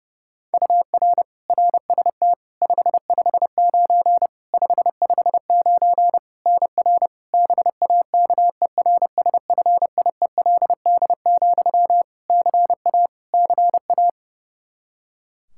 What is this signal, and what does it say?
Morse code, heard as a keyed tone, about 32 wpm, 700 Hz. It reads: UR RST 559 559 NR BAKERSFIELD, CA CA
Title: Practice QSO - Exchange 1 - Beginning - 30wpm